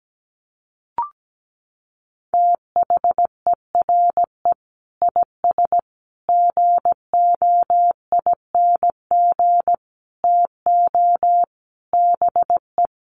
Morse code, heard as a keyed tone, about 17 wpm, 700 Hz. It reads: THERE IS GOING TO BE
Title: Sets of 5 Words Encore 17wpm